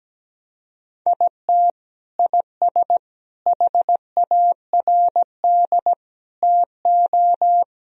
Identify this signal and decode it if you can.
Morse code, heard as a keyed tone, about 17 wpm, 700 Hz. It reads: IT IS HARD TO
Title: Sentences from Top 1000 Words - Review Part 4 - 17wpm